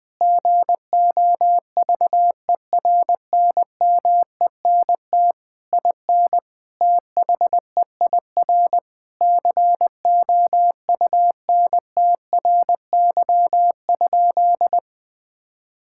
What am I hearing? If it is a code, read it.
Morse code, heard as a keyed tone, about 21 wpm, 700 Hz. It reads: GOVERNMENT IN THEIR COUNTRY?
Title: Sentences from Top 1000 Words - Review Part 5 - 20wpm